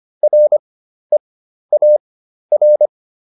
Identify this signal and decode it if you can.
Morse code, heard as a keyed tone, about 25 wpm, spaced as for 12 wpm, 600 Hz. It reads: REAR